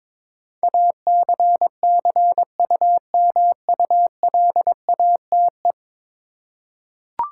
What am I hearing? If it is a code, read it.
Morse code, heard as a keyed tone, about 22 wpm, 700 Hz. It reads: ACCUMULATE
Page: morse-code-ninja-practice-22wpm